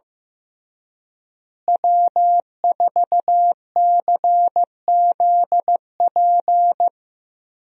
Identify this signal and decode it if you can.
Morse code, heard as a keyed tone, about 15 wpm, 700 Hz. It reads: W4CZP
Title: USA Call Signs - 5 Characters - Repeated 3x - 15wpm